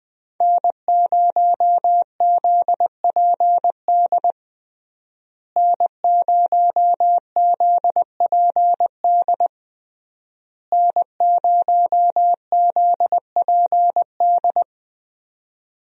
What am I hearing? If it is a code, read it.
Morse code, heard as a keyed tone, about 20 wpm, 700 Hz. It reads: N0ZPD N0ZPD N0ZPD
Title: USA Call Signs - 5 Characters - Repeated 3x - 20wpm